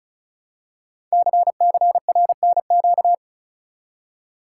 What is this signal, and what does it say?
Morse code, heard as a keyed tone, about 36 wpm, 700 Hz. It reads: CCRNQ
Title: Code Groups - 5 Letter:Numbers - 35wpm